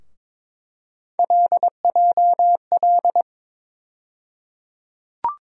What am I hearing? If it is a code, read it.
Morse code, heard as a keyed tone, about 22 wpm, 700 Hz. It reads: LJL
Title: USA Call Sign Postfixes - 3 Letters 22wpm